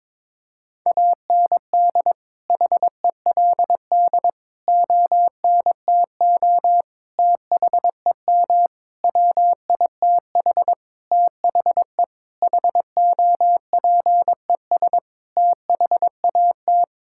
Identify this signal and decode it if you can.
Morse code, heard as a keyed tone, about 22 wpm, 700 Hz. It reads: AND HELD ONTO THEM WITH THE HOPES THAT
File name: Sentences with 9-Word Phrases - 22wpm